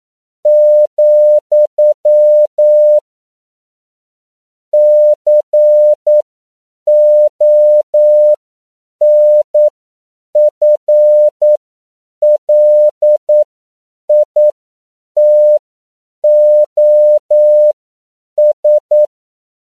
Morse code, heard as a keyed tone, about 9 wpm, spaced as for 7 wpm, 600 Hz. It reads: , CONFLITOS
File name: CW / 10PPM